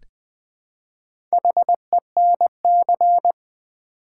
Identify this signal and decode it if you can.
Morse code, heard as a keyed tone, about 20 wpm, 700 Hz. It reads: HENC
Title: Build A Word - 1 Letter At A Time - 3 to 5-Letter Words - 20wpm